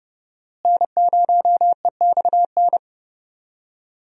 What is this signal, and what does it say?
Morse code, heard as a keyed tone, about 30 wpm, 700 Hz. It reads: N0EXN
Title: USA Call Signs - 5 Characters - Repeated 2x - 30wpm